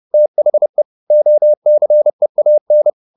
Morse code, heard as a keyed tone, about 30 wpm, 600 Hz. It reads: THE OCEAN
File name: the old man and the sea / the old man and the sea 1 30wpm